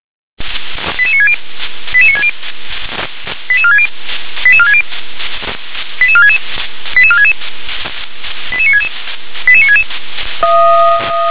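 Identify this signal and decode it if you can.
Five-tone selective call (ZVEI), heard as quick runs of five tones, tones of about 70 ms. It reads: 88960 88960 88460 88460 88466 88466 88966 88966
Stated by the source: ZVEI five-tone selective call 88747 received on an SDR.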